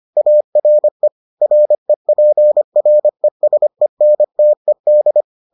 Morse code, heard as a keyed tone, about 25 wpm, 600 Hz. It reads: ARE REPRESENTED